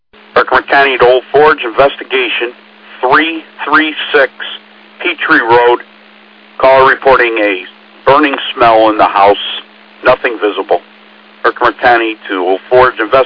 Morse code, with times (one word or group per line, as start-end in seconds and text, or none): none